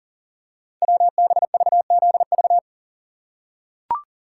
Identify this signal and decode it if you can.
Morse code, heard as a keyed tone, about 39 wpm, 690 Hz. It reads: WBVZV
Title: Code Groups - 5 Letters - 40wpm